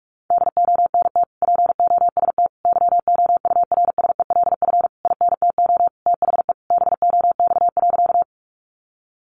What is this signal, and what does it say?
Morse code, heard as a keyed tone, about 44 wpm, 700 Hz. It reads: DONT POST YOURSELF INTO THE BOX.
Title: Sentences from Top 1000 Words 45wpm